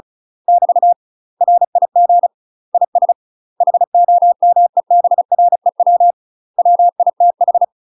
Morse code, heard as a keyed tone, about 36 wpm, 700 Hz. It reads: <BT> RIG IS HOMEBREW WITH
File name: UK MASTER QSOs TRAINING / 35WPM / UK MASTER QSOs 0020